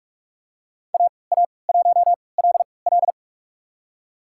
Morse code, heard as a keyed tone, about 46 wpm, spaced as for 25 wpm, 700 Hz. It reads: AA1LL
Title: USA Call Signs - 5 Characters - 3x Character Spacing - 45wpm